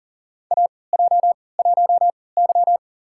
Morse code, heard as a keyed tone, about 40 wpm, spaced as for 23 wpm, 700 Hz. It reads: AJ1Y
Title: USA Call Signs - 4 Characters - 3x Character Spacing - 40wpm